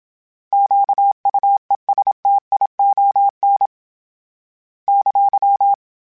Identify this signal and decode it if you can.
Morse code, heard as a keyed tone, about 27 wpm, 800 Hz. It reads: QUESTION !
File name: Misc / Misc-mp3s / Misc @25